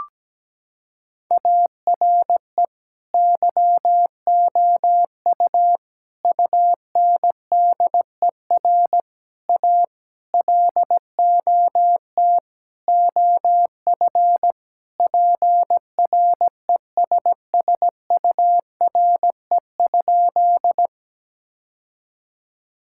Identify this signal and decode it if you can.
Morse code, heard as a keyed tone, about 17 wpm, 700 Hz. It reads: ARE YOU UNDER A LOT OF PRESSURE?
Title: Sentences from Top 1000 Words - Review Part 6 - 17wpm